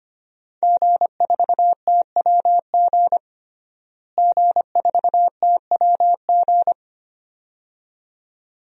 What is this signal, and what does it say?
Morse code, heard as a keyed tone, about 25 wpm, 700 Hz. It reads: G4TWG G4TWG
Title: DXCC-100 Call Signs - 5 Characters - Repeated 2x - 25wpm